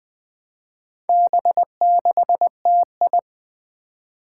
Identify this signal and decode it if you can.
Morse code, heard as a keyed tone, about 20 wpm, 700 Hz. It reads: B6TI
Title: Four Challenging Letter-Number Combinations 20wpm